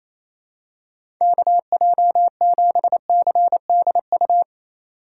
Morse code, heard as a keyed tone, about 28 wpm, 700 Hz. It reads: KJ7CDU
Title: USA Call Signs - 3x Character Spacing - 28wpm